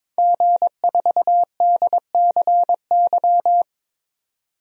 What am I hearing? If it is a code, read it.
Morse code, heard as a keyed tone, about 21 wpm, 700 Hz. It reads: G4DCY
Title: DXCC-100 Call Signs - 5 Characters - 22wpm